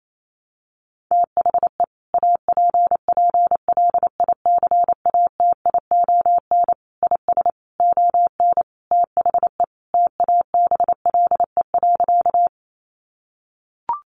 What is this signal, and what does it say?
Morse code, heard as a keyed tone, about 28 wpm, 700 Hz. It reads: THE APPLICATION IS ON THE TABLE.
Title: Sentences from Top 1200 Words 28wpm